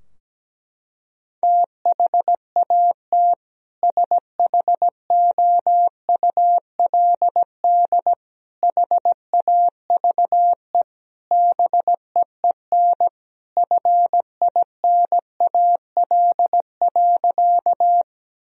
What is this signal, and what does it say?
Morse code, heard as a keyed tone, about 17 wpm, 700 Hz. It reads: THAT SHOULD HAVE BEEN FINAL.
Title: Sentences from Top 1000 Words - Review Part 2 - 17wpm